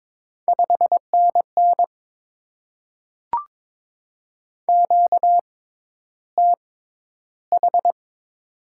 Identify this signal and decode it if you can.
Morse code, heard as a keyed tone, about 22 wpm, spaced as for 11 wpm, 700 Hz. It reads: 5NN QTH
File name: Top 20 QSO Elements - 6x Character Spacing - 22wpm